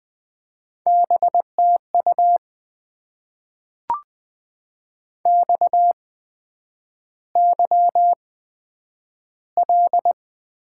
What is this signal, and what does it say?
Morse code, heard as a keyed tone, about 20 wpm, spaced as for 9 wpm, 700 Hz. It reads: BTU XYL
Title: Top 90 QSO Elements - 8x Character Spacing - 20wpm